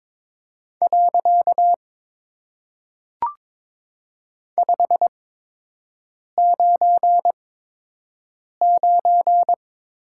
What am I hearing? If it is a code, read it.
Morse code, heard as a keyed tone, about 22 wpm, 700 Hz. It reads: . 5 9 9
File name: Top 90 QSO Elements - 8x Character Spacing - 22wpm